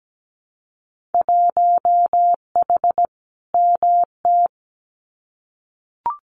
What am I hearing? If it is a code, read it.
Morse code, heard as a keyed tone, about 17 wpm, 700 Hz. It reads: 1H MT
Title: Winter Field Day Exchanges Encore 17wpm